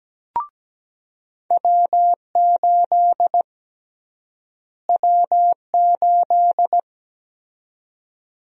Morse code, heard as a keyed tone, about 17 wpm, 700 Hz. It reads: W8 W8
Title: USA Call Sign Prefixes - Repeated 2x - 17wpm